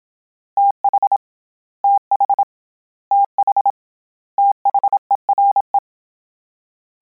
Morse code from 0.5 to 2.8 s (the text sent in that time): TH TH